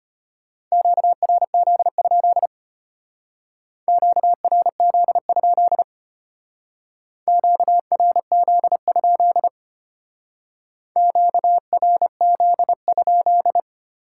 Morse code, heard as a keyed tone, about 33 wpm, 700 Hz. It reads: QRZ? QRZ? QRZ? QRZ?